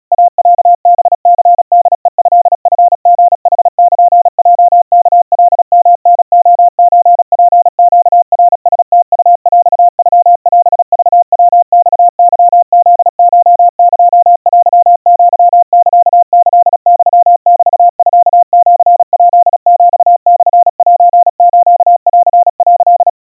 Morse code (Japanese Wabun code), heard as a keyed tone, about 37 wpm, 700 Hz. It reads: イロハニホヘトチリヌルヲワカヨタレソツネナラムウヰノオクヤマケフコエテアサキユメミシヱヒモセスン」
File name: iroha 120 cpm 700 Hz Crispy